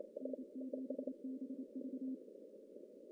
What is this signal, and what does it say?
Morse code, heard as a keyed tone, about 28 wpm, 260 Hz. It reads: N7BV
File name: CW / cqww-cw-k5tr